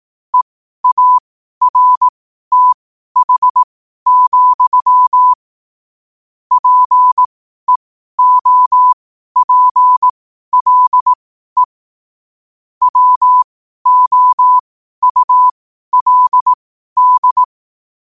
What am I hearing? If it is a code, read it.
Morse code, heard as a keyed tone, about 18 wpm, spaced as for 12 wpm, 1000 Hz. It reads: EARTH, PEOPLE WOULD